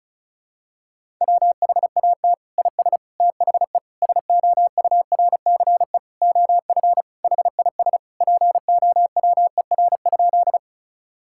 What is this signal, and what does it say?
Morse code, heard as a keyed tone, about 36 wpm, 700 Hz. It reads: WHAT IS THE SOURCE OF HIS POWER?
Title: Sentences from Top 1000 Words - Review Part 5 - 35wpm